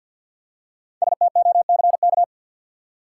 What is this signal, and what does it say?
Morse code, heard as a keyed tone, about 51 wpm, 700 Hz. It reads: STOCK
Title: Five Letter Words - 50wpm